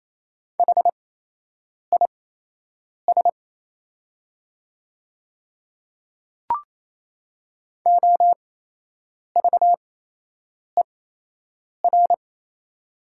Morse code, heard as a keyed tone, about 28 wpm, spaced as for 8 wpm, 700 Hz. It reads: HIS OVER
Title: 8x Character Spacing - Top 100 Words - 28wpm